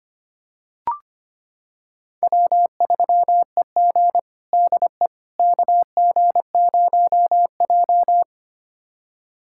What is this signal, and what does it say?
Morse code, heard as a keyed tone, about 25 wpm, 700 Hz. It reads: W3EG DE KG0J
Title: USA Callsign DE Callsign - 4 Characters - 25wpm